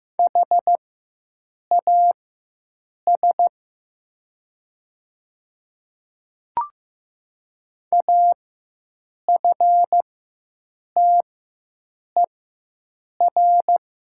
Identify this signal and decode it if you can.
Morse code, heard as a keyed tone, about 15 wpm, spaced as for 7 wpm, 700 Hz. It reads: HAS AFTER
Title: 4x Character Spacing - Top 300 Words - 15wpm